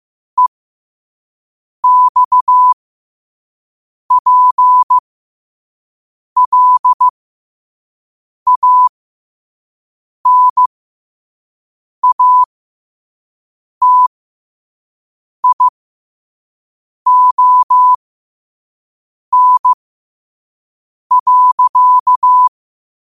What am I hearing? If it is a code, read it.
Morse code, heard as a keyed tone, about 15 wpm, spaced as for 5 wpm, 1000 Hz. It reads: EXPLANATION.
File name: qotdfiles / 5wpm